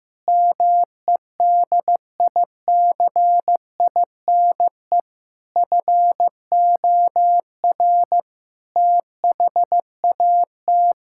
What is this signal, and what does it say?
Morse code, heard as a keyed tone, about 15 wpm, 700 Hz. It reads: MEDICINE FOR THAT
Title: Sentences from Top 1000 Words - Review Part 6 - 15wpm